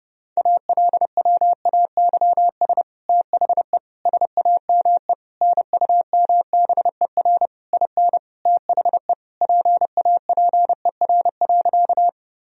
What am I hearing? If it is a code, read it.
Morse code, heard as a keyed tone, about 30 wpm, 700 Hz. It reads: ALWAYS THE SAME NUMBER IN THE PAPER.
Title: Sentences from Top 1000 Words - Review Part 1 - 30wpm